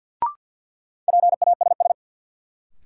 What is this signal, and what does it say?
Morse code, heard as a keyed tone, about 46 wpm, 700 Hz. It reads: PASS
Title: Sentences from Top 400 Words - New Word Drill - 50wpm